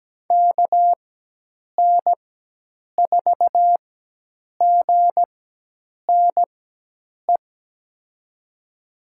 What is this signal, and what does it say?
Morse code, heard as a keyed tone, about 17 wpm, spaced as for 8 wpm, 700 Hz. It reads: KN4GNE
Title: USA Call Signs - 6 Characters - 4x Character Spacing - 17wpm